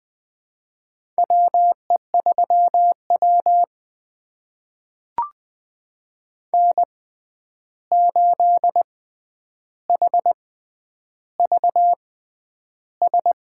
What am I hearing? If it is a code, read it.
Morse code, heard as a keyed tone, about 20 wpm, spaced as for 10 wpm, 700 Hz. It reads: WE3W N8HVS
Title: USA Call Signs - 6x Character Spacing - 20wpm